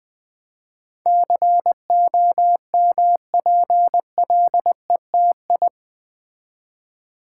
Morse code, heard as a keyed tone, about 20 wpm, 700 Hz. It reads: COMPLETI
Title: Build A Word - 4 Letters At A Time - 10-Letter Words - 20wpm